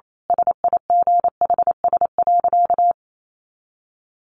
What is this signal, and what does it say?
Morse code, heard as a keyed tone, about 28 wpm, 700 Hz. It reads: SIGHS.
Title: A Sound Of Thunder 28wpm